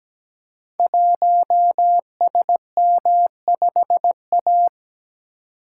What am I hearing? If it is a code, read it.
Morse code, heard as a keyed tone, about 17 wpm, 700 Hz. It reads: 1SM5A
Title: Code Groups - 5 Letter:Numbers - 17wpm